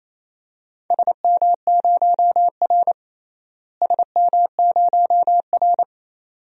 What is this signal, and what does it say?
Morse code, heard as a keyed tone, about 28 wpm, 700 Hz. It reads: SM0R SM0R
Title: DXCC-100 Call Signs - 4 Characters - Repeated 2x - 28wpm